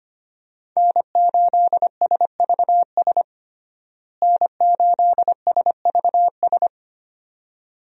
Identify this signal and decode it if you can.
Morse code, heard as a keyed tone, about 25 wpm, 700 Hz. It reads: N8SVS N8SVS
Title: USA Call Signs - 5 Characters - Repeated 2x - 25wpm